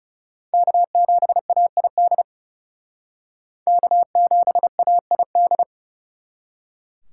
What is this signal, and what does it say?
Morse code, heard as a keyed tone, about 40 wpm, 700 Hz. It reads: K7AID K7AID